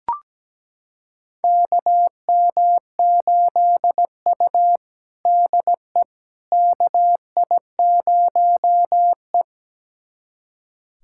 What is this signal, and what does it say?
Morse code, heard as a keyed tone, about 17 wpm, 700 Hz. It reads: KM8U DE KI0E
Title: USA Callsign DE Callsign - 4 Characters - 17wpm